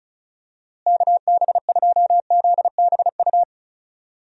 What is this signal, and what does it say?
Morse code, heard as a keyed tone, about 34 wpm, 700 Hz. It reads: KB2ZBU
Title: USA Call Signs - 6 Characters - Repeated 3x - 35wpm